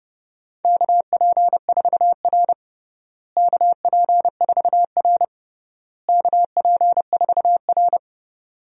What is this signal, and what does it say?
Morse code, heard as a keyed tone, about 30 wpm, 700 Hz. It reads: KP4R KP4R KP4R
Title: DXCC-100 Call Signs - 4 Characters - Repeated 3x - 30wpm